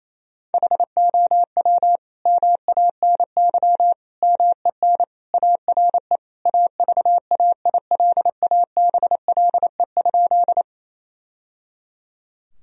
Morse code, heard as a keyed tone, about 28 wpm, 700 Hz. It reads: HOW MANY MEN ARE AVAILABLE?
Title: Sentences from Top 1000 Words - Review Part 3 - 28wpm